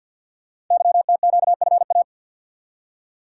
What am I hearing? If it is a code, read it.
Morse code, heard as a keyed tone, about 48 wpm, 700 Hz. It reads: YTQFA
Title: Code Groups - 5 Letters - 50wpm